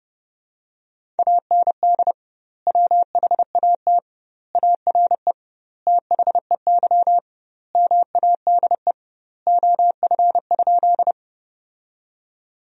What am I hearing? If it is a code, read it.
Morse code, heard as a keyed tone, about 32 wpm, 700 Hz. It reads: AND WHAT ARE THEY MADE OF?
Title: 2x Word-Spacing - Sentences from Top 200 Words - 30wpm